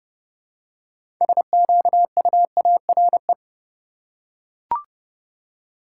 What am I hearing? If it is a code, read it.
Morse code, heard as a keyed tone, about 29 wpm, 700 Hz. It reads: SQUARE
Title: Six Letter Words - 30wpm